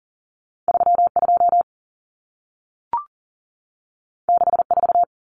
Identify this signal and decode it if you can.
Morse code, heard as a keyed tone, about 40 wpm, 700 Hz. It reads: 32 64
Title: Common Cut Numbers - 2-Digit 40wpm